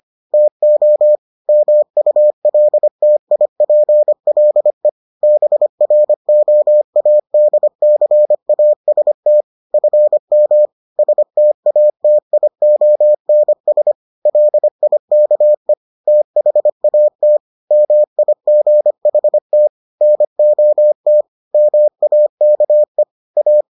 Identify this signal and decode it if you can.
Morse code, heard as a keyed tone, about 25 wpm, 600 Hz. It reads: TO MULTIPLE BROADCAST FM STATIONS LIKE THAT MIGHT NOT MAKE A